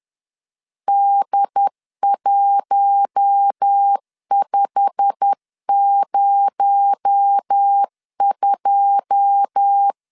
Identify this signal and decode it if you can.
Morse code, heard as a keyed tone, about 10 wpm, 800 Hz. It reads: D1502
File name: LEZIONI / 9 2 10wpm 2736